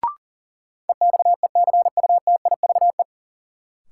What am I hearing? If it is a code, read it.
Morse code, heard as a keyed tone, about 40 wpm, 700 Hz. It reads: EXECUTIVE
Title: Sets of 5 Words Encore - New Word Drill - 40wpm